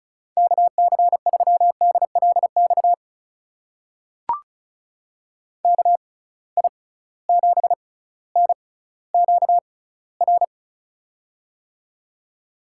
Morse code, heard as a keyed tone, about 36 wpm, spaced as for 18 wpm, 700 Hz. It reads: KC3DLX KI7NQR
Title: USA Call Signs - 6 Characters - 6x Character Spacing - 35wpm